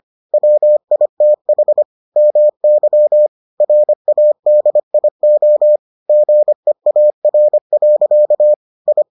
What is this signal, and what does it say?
Morse code, heard as a keyed tone, about 25 wpm, 600 Hz. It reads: WITH MY RADIO GEAR. I